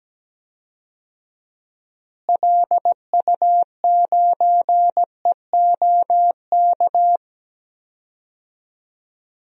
Morse code, heard as a keyed tone, about 17 wpm, 700 Hz. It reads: LU9EOK
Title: DXCC-100 Call Signs - 6 Characters - Repeated 2x - 17wpm